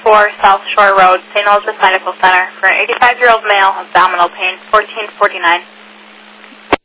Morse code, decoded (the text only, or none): none